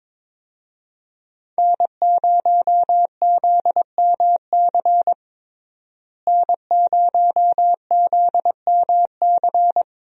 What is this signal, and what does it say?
Morse code, heard as a keyed tone, about 22 wpm, 700 Hz. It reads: N0ZMC N0ZMC
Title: USA Call Signs - 5 Characters - Repeated 3x - 22wpm